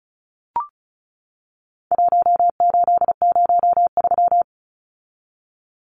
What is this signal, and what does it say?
Morse code, heard as a keyed tone, about 34 wpm, 700 Hz. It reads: 1803
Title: Numbers - 4-Digit 35wpm